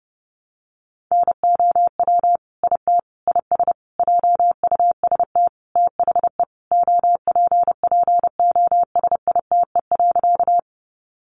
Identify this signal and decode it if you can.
Morse code, heard as a keyed tone, about 30 wpm, 700 Hz. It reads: NOW IT IS JUST THE OPPOSITE.
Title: Sentences from Top 1300 Words 30wpm